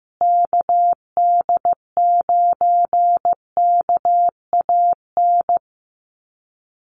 Morse code, heard as a keyed tone, about 15 wpm, 700 Hz. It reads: KD9KAN
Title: USA Call Signs Repeated 3x 15wpm